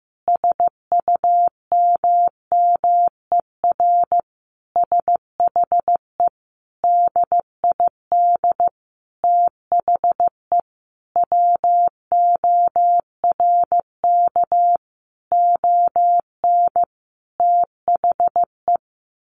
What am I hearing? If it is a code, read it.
Morse code, heard as a keyed tone, about 15 wpm, 700 Hz. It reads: SUMMER SHE DID THE WORK ON THE
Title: Sentences from Top 1200 Words 15wpm